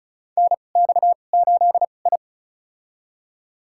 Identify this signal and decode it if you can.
Morse code, heard as a keyed tone, about 36 wpm, spaced as for 25 wpm, 700 Hz. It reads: NX8I
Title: USA Call Signs - 4 Characters - 2x Character Spacing - 35wpm